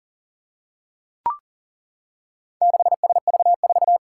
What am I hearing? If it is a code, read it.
Morse code, heard as a keyed tone, about 40 wpm, 700 Hz. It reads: 6SV4
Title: Four Challenging Letter-Number Combinations 40wpm